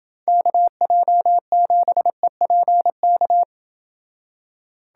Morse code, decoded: KJ7EPK

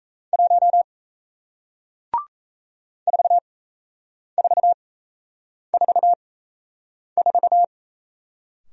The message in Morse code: 1 4444

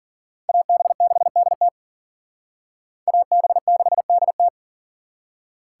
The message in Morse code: AB6DT AB6DT

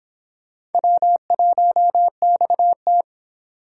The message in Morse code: W1XT

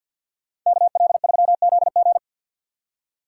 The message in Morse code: KL3ZG

